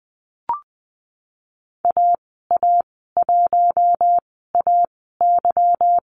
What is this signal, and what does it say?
Morse code, heard as a keyed tone, about 19 wpm, spaced as for 14 wpm, 700 Hz. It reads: AA1AY
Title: 154 Lesson 25g - Letter p - Call Signs 20wpm